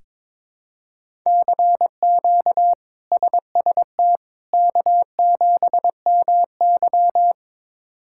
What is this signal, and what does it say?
Morse code, heard as a keyed tone, about 22 wpm, 700 Hz. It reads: CQ SST K7MY
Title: K1USN SST QSO Practice - 22wpm